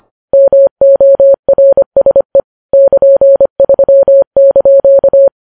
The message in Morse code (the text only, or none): MORSE <KN>3)